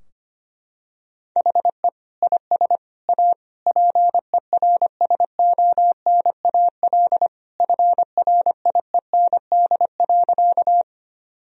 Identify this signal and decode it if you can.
Morse code, heard as a keyed tone, about 25 wpm, 700 Hz. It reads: HE IS A PERSONAL FRIEND.
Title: Sentences from Top 1000 Words - Review Part 4 - 25wpm